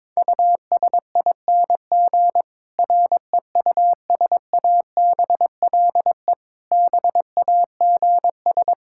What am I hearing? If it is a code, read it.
Morse code, heard as a keyed tone, about 22 wpm, 700 Hz. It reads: USING REUSABLE BAGS
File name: Difficult Sentences with Hints - 22wpm